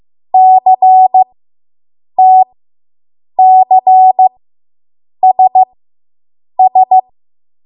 Morse code, heard as a keyed tone, about 15 wpm, spaced as for 7 wpm, 750 Hz. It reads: CTCSS